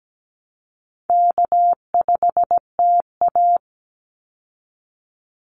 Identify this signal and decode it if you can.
Morse code, heard as a keyed tone, about 17 wpm, 700 Hz. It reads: K5TA
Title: Call Signs Repeated 2x - 17wpm